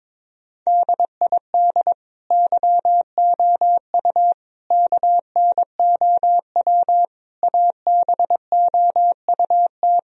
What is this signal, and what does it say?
Morse code, heard as a keyed tone, about 22 wpm, 700 Hz. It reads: DID YOU KNOW ABOUT
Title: Sentences from Top 1000 Words - Review Part 4 - 22wpm